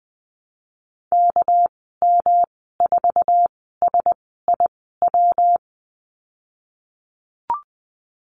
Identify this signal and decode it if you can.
Morse code, heard as a keyed tone, about 20 wpm, spaced as for 15 wpm, 700 Hz. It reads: KM4SIW
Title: 200 Lesson 32g - Letter k - Call Signs 20wpm